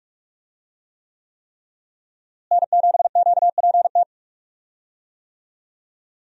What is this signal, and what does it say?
Morse code, heard as a keyed tone, about 46 wpm, 700 Hz. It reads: N7QPT
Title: USA Call Signs - 5 Characters - Repeated 3x - 45wpm